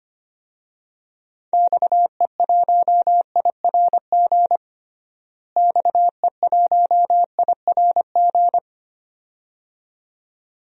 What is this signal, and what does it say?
Morse code, heard as a keyed tone, about 25 wpm, 700 Hz. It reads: XE1IRG XE1IRG
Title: DXCC-100 Call Signs - 6 Characters - Repeated 2x - 25wpm